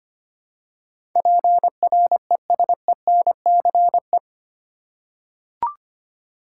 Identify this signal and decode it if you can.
Morse code, heard as a keyed tone, about 25 wpm, 700 Hz. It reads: PRESENCE
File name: Eight Letter Words - 25wpm